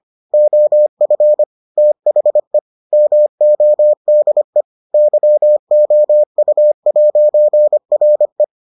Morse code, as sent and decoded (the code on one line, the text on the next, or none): --- ..-. / - .... / -- --- -.. / -.-- --- ..- .----. .-. .
OF THE MODE YOU'RE